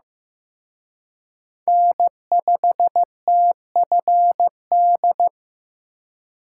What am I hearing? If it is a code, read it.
Morse code, heard as a keyed tone, about 18 wpm, 700 Hz. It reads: N5TFD
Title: Speed Racing - USA 1x3 Call Signs 15wpm